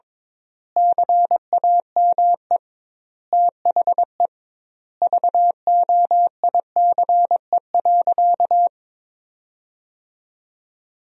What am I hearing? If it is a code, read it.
Morse code, heard as a keyed tone, about 21 wpm, 700 Hz. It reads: CAME THE VOICE.
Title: 2x Word-Spacing - Sentences from Top 500 Words - 22wpm